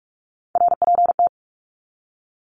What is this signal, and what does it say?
Morse code, heard as a keyed tone, about 46 wpm, 700 Hz. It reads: RPT
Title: Abbreviations 45wpm